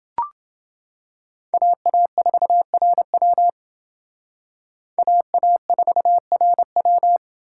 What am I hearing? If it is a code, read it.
Morse code, heard as a keyed tone, about 29 wpm, 700 Hz. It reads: AA4RW AA4RW